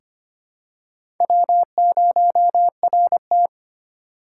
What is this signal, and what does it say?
Morse code, heard as a keyed tone, about 25 wpm, 700 Hz. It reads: W0RT
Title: USA Call Signs - 4x Character Spacing - 25wpm